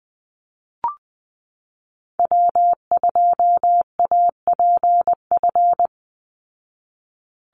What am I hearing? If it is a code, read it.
Morse code, heard as a keyed tone, about 20 wpm, 700 Hz. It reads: W2APF
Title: Call Signs Repeated 2x - 20wpm